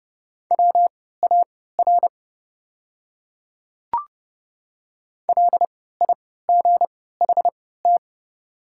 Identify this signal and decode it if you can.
Morse code, heard as a keyed tone, about 32 wpm, spaced as for 17 wpm, 690 Hz. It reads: WAR LIGHT
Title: 3x Character Spacing - Top 500 Words - 30wpm